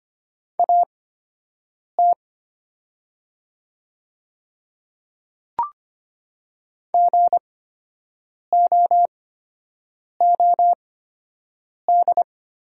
Morse code, heard as a keyed tone, about 25 wpm, spaced as for 7 wpm, 700 Hz. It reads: AT GOOD
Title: 8x Character Spacing - Top 300 Words - 25wpm